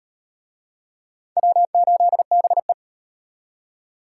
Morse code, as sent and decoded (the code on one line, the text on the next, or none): .-- ---.. -... .
W8BE